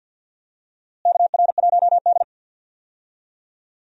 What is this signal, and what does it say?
Morse code, heard as a keyed tone, about 51 wpm, 700 Hz. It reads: KR1D
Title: USA Call Signs - 4 Characters - Repeated 3x - 50wpm